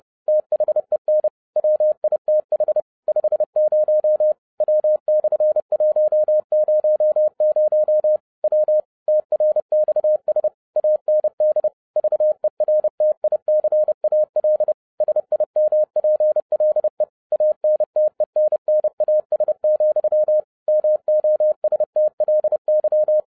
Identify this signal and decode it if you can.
Morse code, heard as a keyed tone, about 30 wpm, 610 Hz. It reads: THEN WITH 50 W/100 W TRXS AND VERTICAL SIMPLE ANTENNAS, MOSTLY
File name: 2012-01-mp3 / n11-I DIDNT MISS THE SOLAR CYCLE 24-30wpm